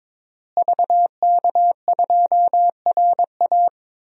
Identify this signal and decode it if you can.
Morse code, heard as a keyed tone, about 22 wpm, 700 Hz. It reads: VK2RA
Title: DXCC-100 Call Signs - 5 Characters - Repeated 3x - 22wpm